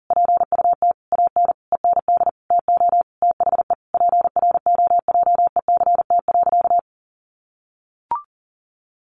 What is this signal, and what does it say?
Morse code, heard as a keyed tone, about 40 wpm, 700 Hz. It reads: PUT AN END TO THE PROJECT.
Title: Sentences from Top 1400 Words 40wpm